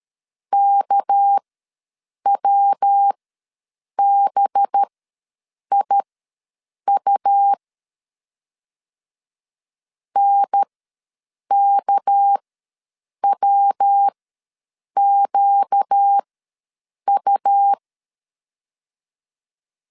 Morse code, heard as a keyed tone, about 13 wpm, spaced as for 7 wpm, 800 Hz. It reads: KWBIU NKWQU
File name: LEZIONI / 6 PKX0 / 6 PK ws 12wpm 3116